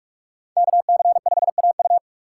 Morse code, heard as a keyed tone, about 43 wpm, 700 Hz. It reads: KC5AU
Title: USA Call Signs - 5 Characters - 4x Character Spacing - 45wpm